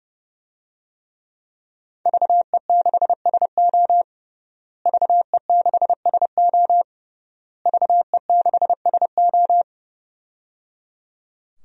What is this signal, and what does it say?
Morse code, heard as a keyed tone, about 30 wpm, 700 Hz. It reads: VE6SO VE6SO VE6SO
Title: DXCC-100 Call Signs - 5 Characters - Repeated 3x - 30wpm